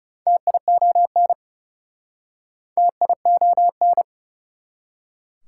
Morse code, heard as a keyed tone, about 37 wpm, 690 Hz. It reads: TION TION
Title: Speed-Racing Prefixes and Suffixes 30wpm